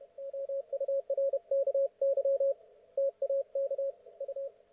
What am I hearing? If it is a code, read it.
Morse code, heard as a keyed tone, about 32 wpm, 550 Hz, with radio noise. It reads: OURKY TAK U